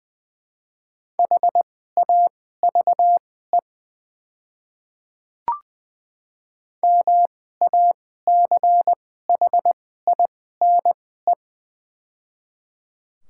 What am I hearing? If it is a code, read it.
Morse code, heard as a keyed tone, about 20 wpm, spaced as for 14 wpm, 700 Hz. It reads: HAVE MACHINE
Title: 2x Character Spacing - Top 500 Words - 20wpm